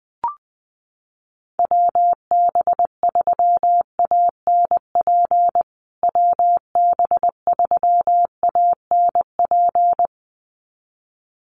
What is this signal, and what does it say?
Morse code, heard as a keyed tone, about 20 wpm, 700 Hz. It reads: WB3ANP WB3ANP
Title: USA Call Signs Repeated 2x 20wpm